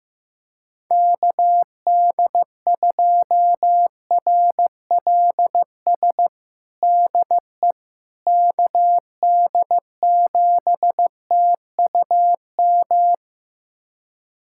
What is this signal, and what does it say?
Morse code, heard as a keyed tone, about 16 wpm, 700 Hz. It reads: KD2RLS DE KD7TUM
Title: USA Callsign DE Callsign - 6 Characters - 15wpm